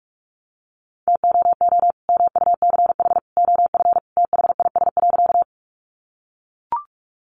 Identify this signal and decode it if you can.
Morse code, heard as a keyed tone, about 44 wpm, 700 Hz. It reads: TOO MUCH OF THIS.
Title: Sentences from Top 200 Words 45wpm